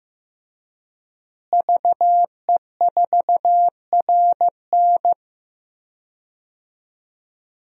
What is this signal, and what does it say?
Morse code, heard as a keyed tone, about 15 wpm, 700 Hz. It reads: VE4RN
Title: DXCC-100 Call Signs - 5 Characters - Repeated 3x - 15wpm